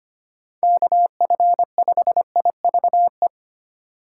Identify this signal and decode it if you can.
Morse code, heard as a keyed tone, about 25 wpm, 700 Hz. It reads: KF5IVE
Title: USA Call Signs - 2x Character Spacing - 25wpm